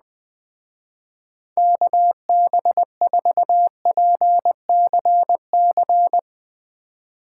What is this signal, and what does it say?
Morse code, heard as a keyed tone, about 20 wpm, 700 Hz. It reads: KB4PCC